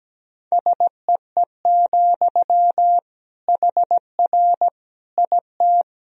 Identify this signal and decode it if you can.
Morse code, heard as a keyed tone, about 17 wpm, 700 Hz. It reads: SEE, HR IT
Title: Sentences from Top 200 Words and Abbreviations 17wpm